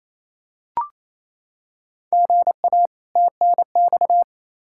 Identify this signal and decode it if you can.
Morse code, heard as a keyed tone, about 28 wpm, 700 Hz. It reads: GA TNX
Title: Practice QSO - Exchange 1 - Beginning - 28wpm